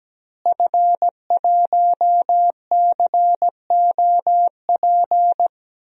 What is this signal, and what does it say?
Morse code, heard as a keyed tone, about 17 wpm, 700 Hz. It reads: F1COP
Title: DXCC-100 Call Signs - 5 Characters - Repeated 3x - 17wpm